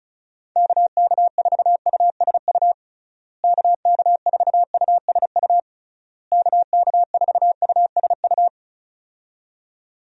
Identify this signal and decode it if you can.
Morse code, heard as a keyed tone, about 34 wpm, 700 Hz. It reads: KK4USU KK4USU KK4USU
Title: USA Call Signs - 6 Characters - Repeated 3x - 35wpm